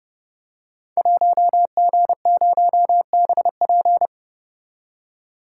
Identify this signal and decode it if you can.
Morse code, heard as a keyed tone, about 30 wpm, 700 Hz. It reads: 1G0BP
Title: Code Groups - 5 Letter:Numbers - 30wpm